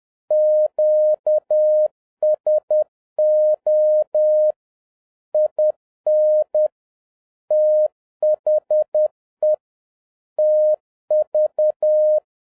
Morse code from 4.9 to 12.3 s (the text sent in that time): IN THE TV